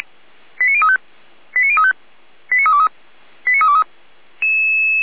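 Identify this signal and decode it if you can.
Five-tone selective call (ZVEI), heard as quick runs of five tones, tones of about 70 ms. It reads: 89035 89035 89323 89323
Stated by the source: ZVEI five-tone selective call received with an SDR.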